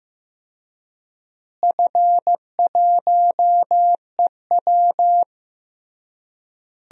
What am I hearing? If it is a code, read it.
Morse code, heard as a keyed tone, about 15 wpm, 700 Hz. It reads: F1EW
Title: DXCC-100 Call Signs - 4 Characters - Repeated 3x - 15wpm